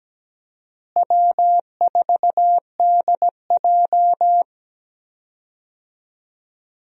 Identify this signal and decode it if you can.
Morse code, heard as a keyed tone, about 17 wpm, 700 Hz. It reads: W4DJ
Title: USA Call Signs - 4 Characters - Repeated 3x - 17wpm